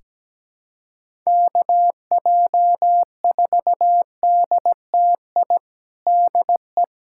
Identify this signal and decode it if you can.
Morse code, heard as a keyed tone, about 17 wpm, 700 Hz. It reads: KJ4DTI DE
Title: USA Callsign DE Callsign 17wpm